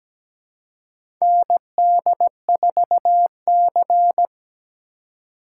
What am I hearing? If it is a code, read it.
Morse code, heard as a keyed tone, about 17 wpm, 700 Hz. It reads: ND4C
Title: USA Call Signs - 4 Characters - 2x Character Spacing - 17wpm